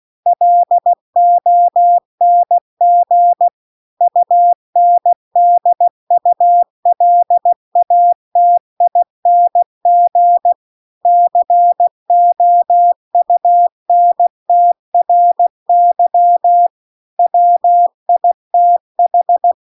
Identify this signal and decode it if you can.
Morse code, heard as a keyed tone, about 16 wpm, 700 Hz. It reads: LONG UNDULATING COUNTRY WITH